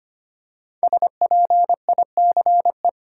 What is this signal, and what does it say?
Morse code, heard as a keyed tone, about 25 wpm, 700 Hz. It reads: SPICE
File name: 5 Letter Words - First 3 Letters Given - 25wpm